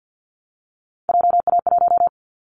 Morse code, heard as a keyed tone, about 50 wpm, 700 Hz. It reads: JA1
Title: Callsign Prefixes for Top 50 DXCC - 50wpm